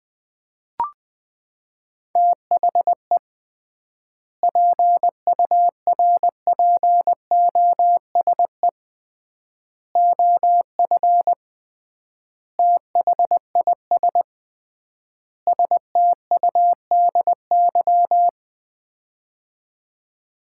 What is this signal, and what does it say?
Morse code, heard as a keyed tone, about 20 wpm, 700 Hz. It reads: THE PURPOSE OF THIS STUDY
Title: 3x Word-Spacing - Sets of 5 Words - 20wpm